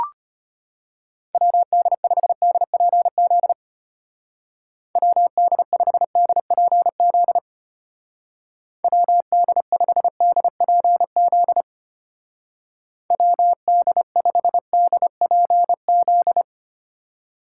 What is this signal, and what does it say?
Morse code, heard as a keyed tone, about 31 wpm, 700 Hz. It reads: WD5DPZ WD5DPZ WD5DPZ WD5DPZ
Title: Speed Racing - USA 2x3 Call Signs 25wpm